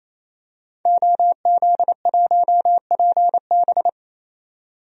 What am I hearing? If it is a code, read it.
Morse code, heard as a keyed tone, about 28 wpm, 700 Hz. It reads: OZ1PB
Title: DXCC-100 Call Signs - 5 Characters - 28wpm